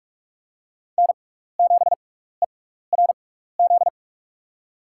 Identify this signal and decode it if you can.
Morse code, heard as a keyed tone, about 41 wpm, spaced as for 15 wpm, 700 Hz. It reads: N7ERZ
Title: USA Call Signs - 6x Character Spacing - 45wpm